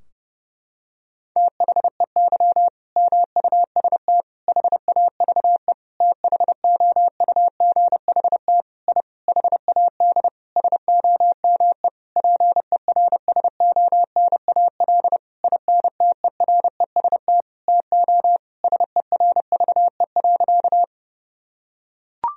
Morse code, heard as a keyed tone, about 30 wpm, 700 Hz. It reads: THEY MUST HAVE THOUGHT I HAD SOME PERSONAL INTEREST TO SERVE.
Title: Sentences from Top 1000 Words - Review Part 4 - 30wpm